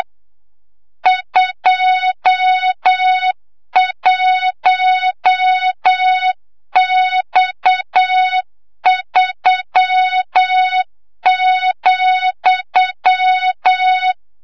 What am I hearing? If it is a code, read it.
Morse code, heard as a keyed tone, about 8 wpm, 3700 Hz, with noise. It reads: 21X3,
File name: Morze-HA2MN / hang / 25 GYAKORLAT